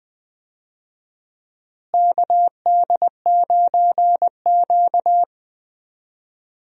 Morse code, heard as a keyed tone, about 21 wpm, 700 Hz. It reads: KD9Q
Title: USA Call Signs - 4 Characters - Repeated 3x - 20wpm